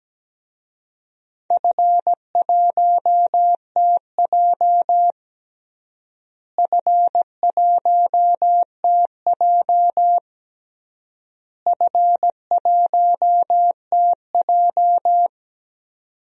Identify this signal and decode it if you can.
Morse code, heard as a keyed tone, about 17 wpm, 700 Hz. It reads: F1TJ F1TJ F1TJ
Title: DXCC-100 Call Signs - 4 Characters - Repeated 3x - 17wpm